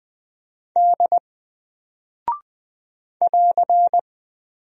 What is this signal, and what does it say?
Morse code, heard as a keyed tone, about 20 wpm, 700 Hz. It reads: D <AR>
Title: Single Character (Letter:Number:Punctuation:Prosign) Alt 20wpm